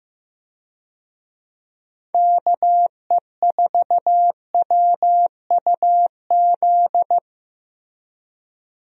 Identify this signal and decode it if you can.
Morse code, heard as a keyed tone, about 16 wpm, 700 Hz. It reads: KE4WUZ
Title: USA Call Signs - 6 Characters - Repeated 3x - 15wpm